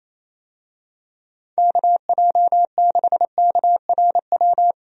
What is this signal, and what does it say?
Morse code, heard as a keyed tone, about 28 wpm, 700 Hz. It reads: KJ6KRW
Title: USA Call Signs - 6 Characters - Repeated 3x - 28wpm